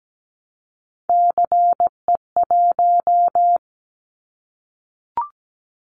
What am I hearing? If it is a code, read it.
Morse code, heard as a keyed tone, about 17 wpm, 700 Hz. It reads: CE1
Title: Callsign Prefixes for Top 100 DXCC - 17wpm